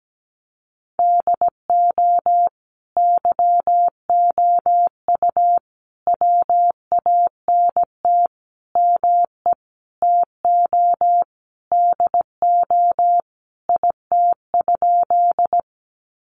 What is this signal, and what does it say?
Morse code, heard as a keyed tone, about 17 wpm, 700 Hz. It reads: DO YOU WANT ME TO DO IT?
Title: Sentences from Top 200 Words 17wpm